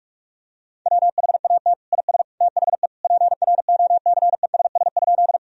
Morse code, heard as a keyed tone, about 46 wpm, 700 Hz. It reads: WHAT IS THE PROCESS?
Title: Sentences from Top 1000 Words - Review Part 3 - 45wpm